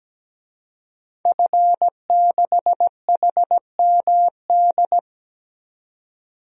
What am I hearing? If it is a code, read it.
Morse code, heard as a keyed tone, about 17 wpm, 700 Hz. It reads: F6HMD
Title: DXCC-100 Call Signs - 5 Characters - 17wpm